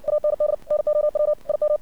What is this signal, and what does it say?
Morse code, heard as a keyed tone, about 60 wpm, 600 Hz. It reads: AND NOW IM
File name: archived / CRSnet